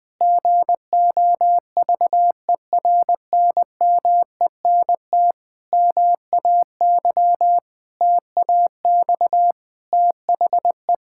Morse code, heard as a keyed tone, about 20 wpm, 700 Hz. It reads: GOVERNMENT MAY TAX THE
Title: Sentences from Top 1000 Words - Review Part 5 - 20wpm